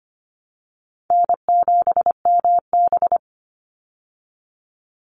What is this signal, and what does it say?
Morse code, heard as a keyed tone, about 25 wpm, 700 Hz. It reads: N7MB
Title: Easy Call Signs Repeated 2x - 25wpm